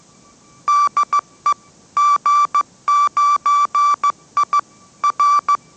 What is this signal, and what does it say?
Morse code, heard as a keyed tone, about 15 wpm, 1160 Hz. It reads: DE G9I R
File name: mrx / mrx065 16wpm